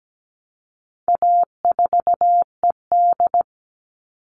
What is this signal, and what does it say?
Morse code, heard as a keyed tone, about 17 wpm, 700 Hz. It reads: A4ED
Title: All Cut Numbers - 4-Digit 17wpm